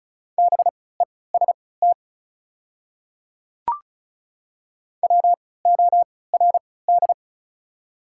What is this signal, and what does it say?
Morse code, heard as a keyed tone, about 36 wpm, spaced as for 20 wpm, 700 Hz. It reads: BEST WORD
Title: 3x Character Spacing - Top 500 Words - 35wpm